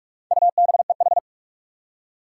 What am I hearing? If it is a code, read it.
Morse code, heard as a keyed tone, about 46 wpm, 700 Hz. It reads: UBEH
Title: Four Challenging Letter-Number Combinations 45wpm